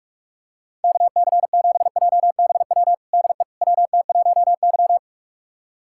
Morse code, heard as a keyed tone, about 41 wpm, 700 Hz. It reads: KC7JBW DE WT1Y
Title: USA Callsign DE Callsign 45wpm